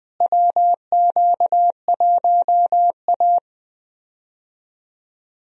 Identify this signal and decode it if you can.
Morse code, heard as a keyed tone, about 20 wpm, 700 Hz. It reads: WQ1A
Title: USA Call Signs - 4 Characters - Repeated 2x - 20wpm